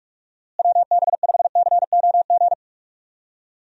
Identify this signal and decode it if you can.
Morse code, heard as a keyed tone, about 46 wpm, 700 Hz. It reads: WB5COG
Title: USA Call Signs - 6 Characters - Repeated 3x - 45wpm